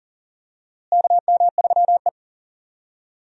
Morse code, heard as a keyed tone, about 40 wpm, 700 Hz. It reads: KM3E
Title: USA Call Signs - 4 Characters - Repeated 3x - 40wpm